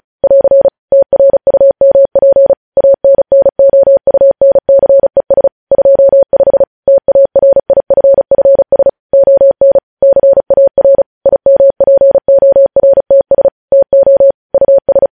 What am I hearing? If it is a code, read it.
Morse code, heard as a keyed tone, about 36 wpm, 550 Hz. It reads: <AR> TRUMP ANNOUNCES 25 TARIFFS ON CAR IMPORTS TO US
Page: News headlines in Morse code at 35 wpm